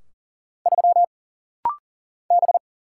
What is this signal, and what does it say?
Morse code, heard as a keyed tone, about 40 wpm, 700 Hz. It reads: 3 B
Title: Single Letter-Number Alt - Rapid-Fire 40wpm